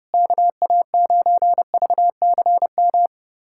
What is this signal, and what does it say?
Morse code, heard as a keyed tone, about 30 wpm, 700 Hz. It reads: KA9VCM
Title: USA Callsign DE Callsign 30wpm